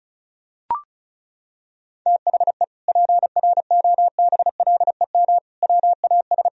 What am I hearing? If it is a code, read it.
Morse code, heard as a keyed tone, about 36 wpm, 700 Hz. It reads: THE PROBLEM WAS
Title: Sentences from Top 1000 Words - Review Part 2 - 35wpm